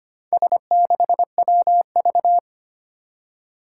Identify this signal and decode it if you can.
Morse code, heard as a keyed tone, about 25 wpm, 700 Hz. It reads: S6WV
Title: Four Challenging Letter-Number Combinations 25wpm